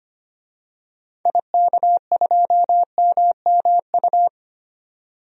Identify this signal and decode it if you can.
Morse code, heard as a keyed tone, about 25 wpm, 700 Hz. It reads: IK2MMU
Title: DXCC-100 Call Signs - 6 Characters - 25wpm